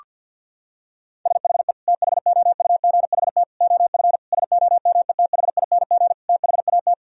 Morse code, heard as a keyed tone, about 52 wpm, 700 Hz. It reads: SHE THOUGHT OF SOMETHING THAT
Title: Sentences from Top 1000 Words - Review Part 1 - 50wpm